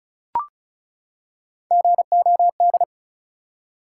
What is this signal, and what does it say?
Morse code, heard as a keyed tone, about 36 wpm, 700 Hz. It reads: GOD
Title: Three Letter Words - 35wpm